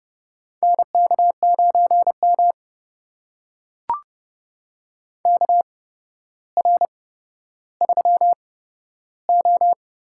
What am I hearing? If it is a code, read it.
Morse code, heard as a keyed tone, about 30 wpm, spaced as for 13 wpm, 700 Hz. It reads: NK9M KR3O
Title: USA Call Signs - 4 Characters - 8x Character Spacing - 30wpm